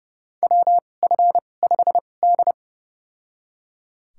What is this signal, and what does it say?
Morse code, heard as a keyed tone, about 30 wpm, spaced as for 22 wpm, 700 Hz. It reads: WF5D
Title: USA Call Signs - 2x Character Spacing - 30wpm